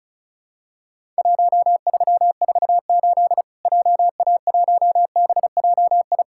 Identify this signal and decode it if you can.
Morse code, heard as a keyed tone, about 34 wpm, 700 Hz. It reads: 1348 JA1BJI
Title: Worked All Europe DX Contest - QTC Exchange - 35wpm